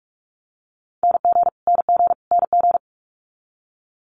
Morse code, heard as a keyed tone, about 46 wpm, 700 Hz. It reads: NG NG NG
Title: Multi-letter Phonemes Repeated 3x 45wpm